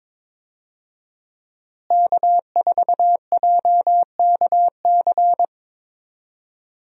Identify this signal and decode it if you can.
Morse code, heard as a keyed tone, about 22 wpm, 700 Hz. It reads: K4JKC
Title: USA Call Signs - 5 Characters - Repeated 3x - 22wpm